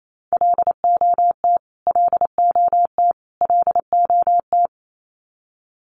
Morse code, heard as a keyed tone, about 28 wpm, 700 Hz. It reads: LOT LOT LOT
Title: Top 500 Words Repeated 28wpm